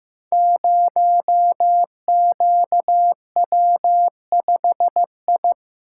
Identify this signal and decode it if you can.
Morse code, heard as a keyed tone, about 15 wpm, 700 Hz. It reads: 0QW5I
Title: Code Groups - 5 Letter:Numbers - 15wpm